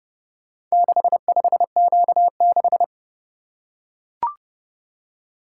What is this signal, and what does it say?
Morse code, heard as a keyed tone, about 29 wpm, 700 Hz. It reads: 65Q6